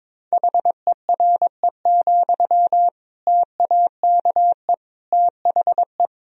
Morse code, heard as a keyed tone, about 22 wpm, 700 Hz. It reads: HERE, TAKE THE
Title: Sentences from Top 1000 Words - Review Part 1 - 22wpm